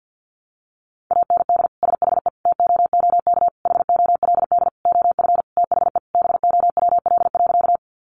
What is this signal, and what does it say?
Morse code, heard as a keyed tone, about 48 wpm, 700 Hz. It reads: AND SHE TOOK HOLD OF THE BOWL.
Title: Sentences from Top 1200 Words 50wpm